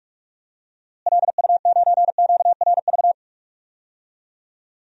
Morse code, heard as a keyed tone, about 46 wpm, 700 Hz. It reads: LU9QRV
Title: DXCC-100 Call Signs - 6 Characters - Repeated 2x - 45wpm